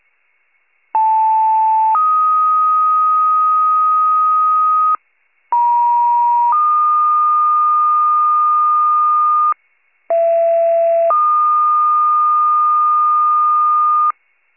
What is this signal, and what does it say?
Two-tone page, heard as a one-second tone, then a three-second tone, about 880 then 1250 Hz, about 950 then 1220 Hz, about 670 then 1220 Hz. It is heard from 0.9 to 14.1 s.
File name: pager / 76 FIRE / wcalert